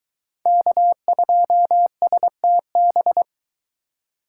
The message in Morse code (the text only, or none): K2STB